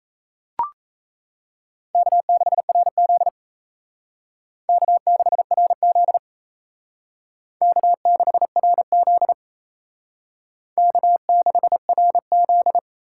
Morse code, heard as a keyed tone, about 34 wpm, 700 Hz. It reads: K6RZ K6RZ K6RZ K6RZ
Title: Speed Racing - USA 1x2 Call Signs 28wpm